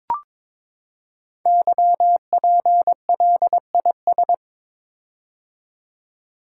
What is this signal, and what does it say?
Morse code, heard as a keyed tone, about 23 wpm, 690 Hz. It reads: YPLIS
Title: Code Groups - 5 Letters - 22wpm